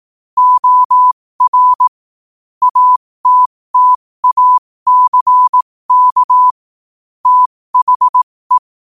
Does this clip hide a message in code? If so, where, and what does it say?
Yes: Morse code from 0.4 to 8.6 s, heard as a keyed tone, about 18 wpm, spaced as for 15 wpm, 1000 Hz. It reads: OR ATTACK THE